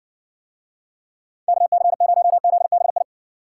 Morse code, heard as a keyed tone, about 60 wpm, 700 Hz. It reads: KC0ZBI